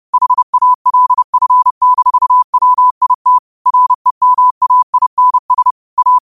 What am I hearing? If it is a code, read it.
Morse code, heard as a keyed tone, about 30 wpm, 1000 Hz. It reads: HALF-WIT REMAINS A
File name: qotdfiles / 30wpm